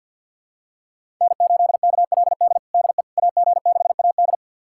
Morse code, heard as a keyed tone, about 53 wpm, 700 Hz. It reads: N8KLD DE AG6AD
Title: USA Callsign DE Callsign - 5 Characters - 50wpm